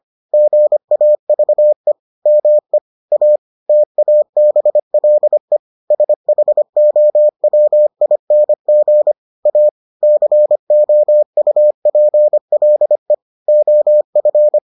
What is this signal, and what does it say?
Morse code, heard as a keyed tone, about 25 wpm, 600 Hz. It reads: GAVE ME A TABLE SHOWING A COUPLE OF